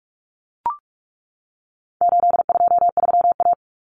Morse code, heard as a keyed tone, about 44 wpm, 700 Hz. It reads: 823A
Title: Common Cut Numbers - 4-Digit 45wpm